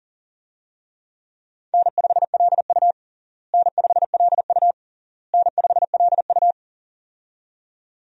Morse code, heard as a keyed tone, about 40 wpm, 700 Hz. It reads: N5LU N5LU N5LU
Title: USA Call Signs - 4 Characters - Repeated 3x - 40wpm